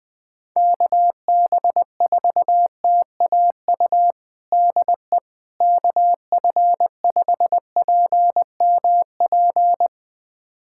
Morse code, heard as a keyed tone, about 20 wpm, 700 Hz. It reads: KB4TAU DE KF5PMP
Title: USA Callsign DE Callsign - 6 Characters - 20wpm